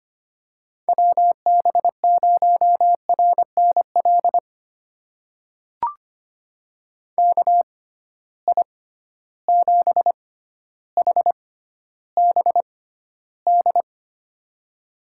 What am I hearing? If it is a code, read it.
Morse code, heard as a keyed tone, about 25 wpm, spaced as for 13 wpm, 700 Hz. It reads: WB0RNL KI7HBD
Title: USA Call Signs - 6 Characters - 6x Character Spacing - 25wpm